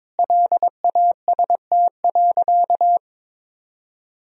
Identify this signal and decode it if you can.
Morse code, heard as a keyed tone, about 21 wpm, 700 Hz. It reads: LAST.
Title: Sentences from Top 300 Words and Abbreviations 22wpm